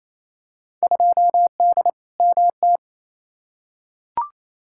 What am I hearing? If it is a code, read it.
Morse code, heard as a keyed tone, about 28 wpm, 700 Hz. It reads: 2D MT
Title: Field Day Exchanges Encore - 28wpm